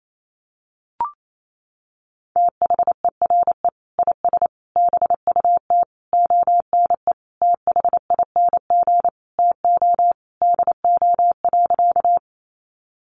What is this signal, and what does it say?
Morse code, heard as a keyed tone, about 28 wpm, 700 Hz. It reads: THERE IS BUT ONE THING TO DO.
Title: Sentences from Top 100 Words 28wpm